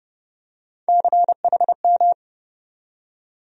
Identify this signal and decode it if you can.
Morse code, heard as a keyed tone, about 30 wpm, 700 Hz. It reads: CHM
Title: USA Call Sign Postfixes - 3 Letters 30wpm